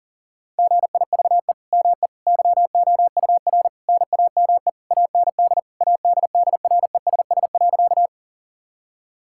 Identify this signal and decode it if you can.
Morse code, heard as a keyed tone, about 40 wpm, 700 Hz. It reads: GIVE ME YOUR NAME AND ADDRESS.
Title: Sentences from Top 1000 Words - Review Part 6 - 40wpm